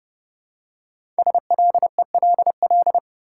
Morse code, heard as a keyed tone, about 30 wpm, 700 Hz. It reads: SLELL
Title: Code Groups - 5 Letters - 30wpm